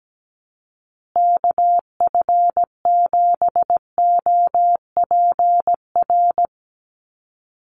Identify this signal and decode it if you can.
Morse code, heard as a keyed tone, about 17 wpm, 700 Hz. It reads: KF7OPR
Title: USA Call Signs Repeated 2x 17wpm